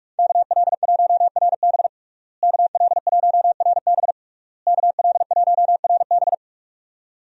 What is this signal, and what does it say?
Morse code, heard as a keyed tone, about 46 wpm, 700 Hz. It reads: KL1RB KL1RB KL1RB
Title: USA Call Signs - 5 Characters - Repeated 3x - 45wpm